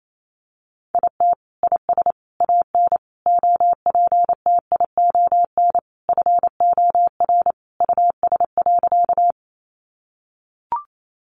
Morse code, heard as a keyed tone, about 28 wpm, 700 Hz. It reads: IT IS AN OPTION FOR US.
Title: Sentences from Top 1300 Words 28wpm